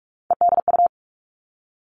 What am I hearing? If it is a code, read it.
Morse code, heard as a keyed tone, about 46 wpm, 700 Hz. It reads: EDU
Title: Three Challenging Letter-Number Combinations 45wpm